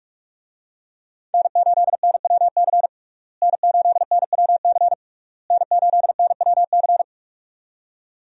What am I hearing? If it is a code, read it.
Morse code, heard as a keyed tone, about 42 wpm, 700 Hz. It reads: N8NWC N8NWC N8NWC
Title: USA Call Signs - 5 Characters - Repeated 3x - 45wpm